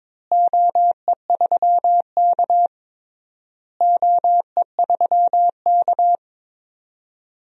Morse code, heard as a keyed tone, about 22 wpm, 700 Hz. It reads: OE3K OE3K
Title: DXCC-100 Call Signs - 4 Characters - Repeated 3x - 22wpm